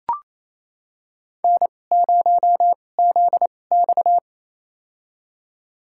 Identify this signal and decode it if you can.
Morse code, heard as a keyed tone, about 28 wpm, spaced as for 20 wpm, 700 Hz. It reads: N0ZX
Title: USA Call Signs - 2x Character Spacing - 28wpm